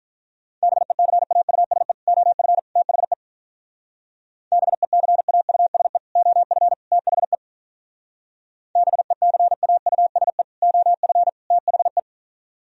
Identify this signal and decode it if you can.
Morse code, heard as a keyed tone, about 44 wpm, 700 Hz. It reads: BECAUSE OF THE BECAUSE OF THE BECAUSE OF THE